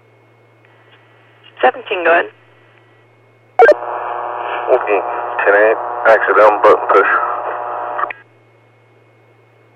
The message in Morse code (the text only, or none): none